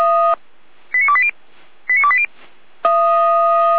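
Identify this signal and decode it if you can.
Five-tone selective call (ZVEI), heard as quick runs of five tones, tones of about 70 ms. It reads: 89280 89280
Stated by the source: ZVEI five-tone selective call received on an SDR.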